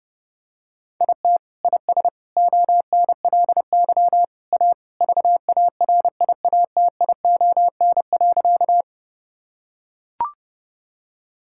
Morse code, heard as a keyed tone, about 30 wpm, 700 Hz. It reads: IT IS ONLY A VARIATION.
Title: Sentences from Top 1600 Words 30wpm